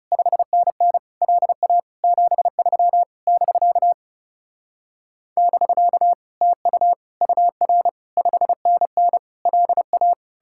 Morse code, heard as a keyed tone, about 40 wpm, 700 Hz. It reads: 5NN LA 73 <BK> <BK> TU UR 5NN LA